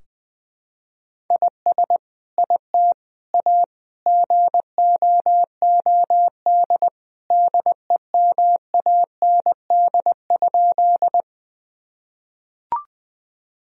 Morse code, heard as a keyed tone, about 20 wpm, 700 Hz. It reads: IS IT A GOOD DEMAND?
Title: Sentences from Top 1000 Words - Review Part 4 - 20wpm